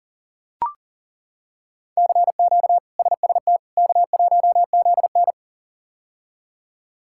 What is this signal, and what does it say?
Morse code, heard as a keyed tone, about 40 wpm, 700 Hz. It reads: CQ SST K1ZN
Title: K1USN SST QSO Practice - 40wpm